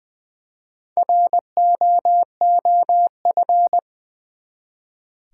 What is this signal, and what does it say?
Morse code, heard as a keyed tone, about 20 wpm, 700 Hz. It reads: ROOF
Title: Four Letter Words - 20wpm